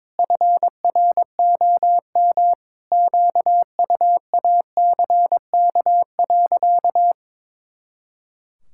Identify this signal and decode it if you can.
Morse code, heard as a keyed tone, about 22 wpm, 700 Hz. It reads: FROM QUACK.
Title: Pangram Challenge v2 - 22wpm